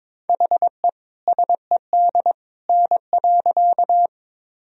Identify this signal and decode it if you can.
Morse code, heard as a keyed tone, about 22 wpm, 700 Hz. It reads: HE SED N.
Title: Sentences from Top 100 Words and Abbreviations 22wpm